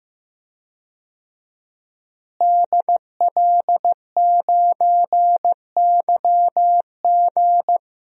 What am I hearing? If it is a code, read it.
Morse code, heard as a keyed tone, about 15 wpm, 700 Hz. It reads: DL9YG
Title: DXCC-100 Call Signs - 5 Characters - Repeated 2x - 15wpm